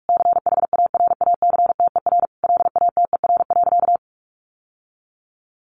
Morse code, heard as a keyed tone, about 44 wpm, 700 Hz. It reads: CHARACTER LATER.
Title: Sentences from Top 1200 Words 45wpm